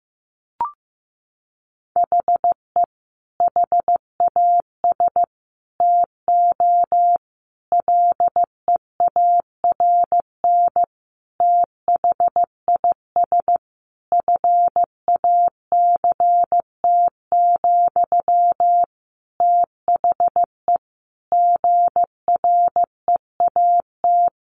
Morse code, heard as a keyed tone, about 15 wpm, 700 Hz. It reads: HE HAS TO LEARN THIS FACT, THE GREAT
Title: Sentences from Top 1200 Words 15wpm